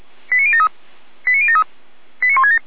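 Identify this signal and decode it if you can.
Five-tone selective call (ZVEI), heard as quick runs of five tones, tones of about 70 ms. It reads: 89073 89073 89167
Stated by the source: ZVEI five-tone call received by SDR.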